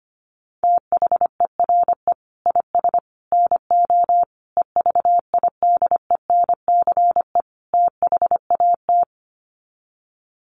Morse code, heard as a keyed tone, about 25 wpm, 700 Hz. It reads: THERE IS NO EVIDENCE THAT
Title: Sets of 5 Words Encore 25wpm